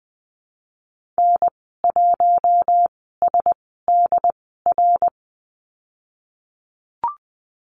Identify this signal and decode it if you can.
Morse code, heard as a keyed tone, about 20 wpm, spaced as for 15 wpm, 700 Hz. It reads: N1SDR
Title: 076 Lesson 13g - Letter L - Call Signs 20wpm